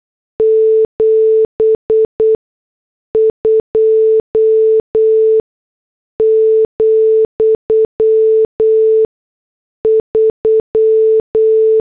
Morse code, heard as a keyed tone, about 8 wpm, spaced as for 6 wpm, 440 Hz. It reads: 72,3